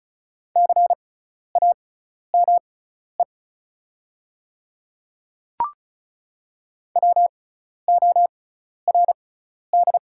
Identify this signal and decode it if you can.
Morse code, heard as a keyed tone, about 36 wpm, spaced as for 12 wpm, 690 Hz. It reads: CAME WORD
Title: 6x Character Spacing - Top 300 Words - 35wpm